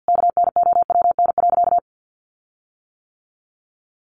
Morse code, heard as a keyed tone, about 48 wpm, 700 Hz. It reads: KNOWN.
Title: Sentences from Top 1300 Words 50wpm